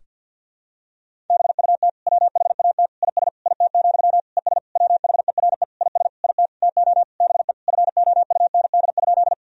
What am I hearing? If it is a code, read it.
Morse code, heard as a keyed tone, about 53 wpm, 700 Hz. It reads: BUT WHAT IS IT, ES WHERE IS IT TO BE FOUND?
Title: Sentences from Top 300 Words and Abbreviations 50wpm